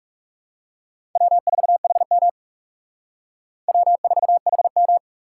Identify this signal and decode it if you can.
Morse code, heard as a keyed tone, about 37 wpm, 700 Hz. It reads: W4HM W4HM
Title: Speed Racing - USA Call Signs 30wpm